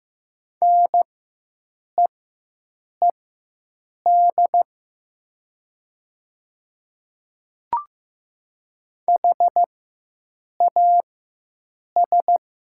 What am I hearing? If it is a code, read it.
Morse code, heard as a keyed tone, about 15 wpm, spaced as for 7 wpm, 700 Hz. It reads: NEED HAS
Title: 4x Character Spacing - Top 200 Words - 15wpm